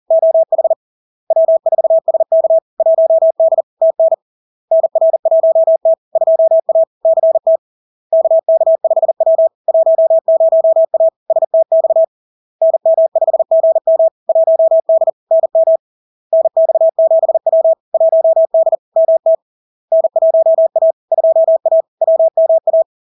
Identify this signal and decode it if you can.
Morse code, heard as a keyed tone, about 40 wpm, 650 Hz. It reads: OH W4SK 1D TN NR1T 2A CT KK5W 10A STX NM5GM 1D NM NX7W 1D MT N1A 2A WMA
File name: LARK-FD-2022-Log-in-CW-40wpm-650-Hz